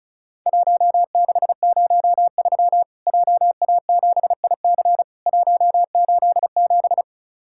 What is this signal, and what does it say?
Morse code, heard as a keyed tone, about 34 wpm, 700 Hz. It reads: 1603 JA7IC 187
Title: Worked All Europe DX Contest - QTC Exchange - 35wpm